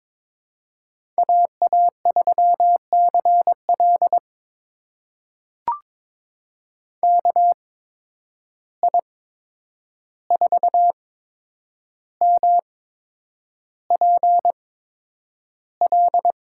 Morse code, heard as a keyed tone, about 21 wpm, spaced as for 9 wpm, 700 Hz. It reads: AA3CL KI4MPL
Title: USA Call Signs - 8x Character Spacing - 22wpm